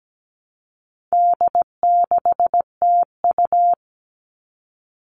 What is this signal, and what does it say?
Morse code, heard as a keyed tone, about 17 wpm, 700 Hz. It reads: D6TU
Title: All Cut Numbers - 4-Digit 17wpm